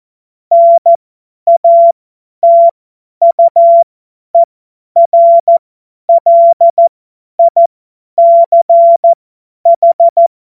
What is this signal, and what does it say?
Morse code, heard as a keyed tone, about 14 wpm, spaced as for 10 wpm, 680 Hz. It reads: NATUERLICH
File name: Morsekurs / Witze / Witz 10 Juni 2025 50 F70